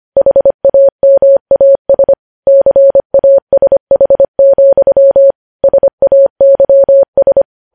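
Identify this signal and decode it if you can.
Morse code, heard as a keyed tone, about 25 wpm, 550 Hz. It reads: HAMAS CASH, SAYS
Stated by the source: Morse code news headlines sent at 25.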